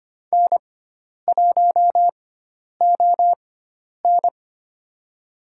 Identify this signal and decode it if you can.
Morse code, heard as a keyed tone, about 25 wpm, spaced as for 10 wpm, 700 Hz. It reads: N1ON
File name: Lesson 16g - Letter c - Call Signs 25wpm